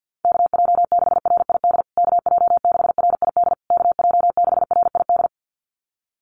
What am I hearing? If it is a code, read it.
Morse code, heard as a keyed tone, about 50 wpm, 700 Hz. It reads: KJ6RID KJ6RID KJ6RID
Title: USA Call Signs Repeated 3x 50wpm